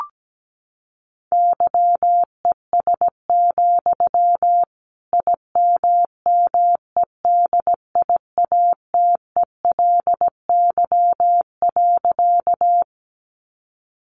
Morse code, heard as a keyed tone, about 17 wpm, 700 Hz. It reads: YES, IMMEDIATELY.
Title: Sentences from Top 1100 Words 17wpm